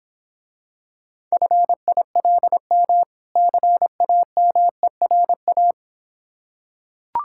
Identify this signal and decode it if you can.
Morse code, heard as a keyed tone, about 26 wpm, 700 Hz. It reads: FILM CAMERA